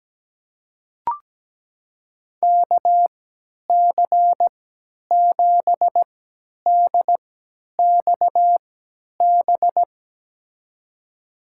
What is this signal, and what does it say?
Morse code, heard as a keyed tone, about 17 wpm, spaced as for 10 wpm, 700 Hz. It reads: KC7DXB
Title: USA Call Signs - 3x Character Spacing - 17wpm